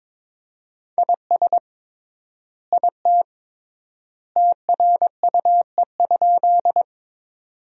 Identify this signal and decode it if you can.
Morse code, heard as a keyed tone, about 22 wpm, 700 Hz. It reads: IS IT TRUE?
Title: 3x Word-Spacing - Sentences from Top 400 Words - 22wpm